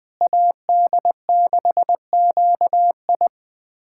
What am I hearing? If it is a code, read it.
Morse code, heard as a keyed tone, about 20 wpm, 700 Hz. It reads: AD6QI
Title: USA Callsign DE Callsign - 5 Characters - 20wpm